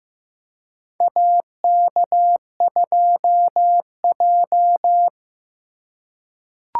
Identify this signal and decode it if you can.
Morse code, heard as a keyed tone, about 15 wpm, 700 Hz. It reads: AK2J
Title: USA Call Signs - 4 Characters - 2x Character Spacing - 15wpm